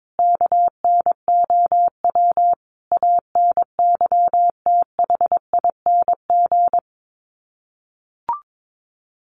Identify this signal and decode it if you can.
Morse code, heard as a keyed tone, about 22 wpm, 700 Hz. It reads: KNOW ANYTHING
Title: Sets of 5 Words Encore 22wpm